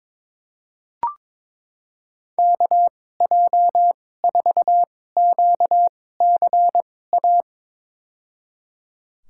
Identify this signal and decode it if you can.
Morse code, heard as a keyed tone, about 22 wpm, spaced as for 16 wpm, 700 Hz. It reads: KJ4QCA
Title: USA Call Signs - 6 Characters - 2x Character Spacing - 22wpm